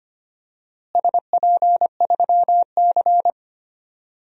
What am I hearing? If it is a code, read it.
Morse code, heard as a keyed tone, about 25 wpm, 700 Hz. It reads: SP3C
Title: DXCC-100 Call Signs - 4 Characters - Repeated 3x - 25wpm